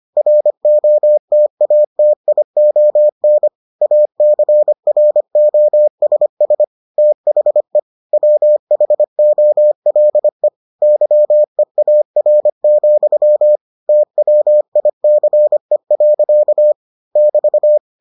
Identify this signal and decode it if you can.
Morse code, heard as a keyed tone, about 25 wpm, 600 Hz. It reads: ROTATION ACROSS THE WHOLE YEAR, TWICE. <BT>